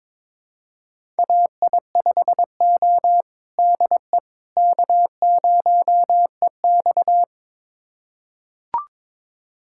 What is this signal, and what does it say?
Morse code, heard as a keyed tone, about 22 wpm, 700 Hz. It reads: AI5O DE K0EX
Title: USA Callsign DE Callsign - 4 Characters - 22wpm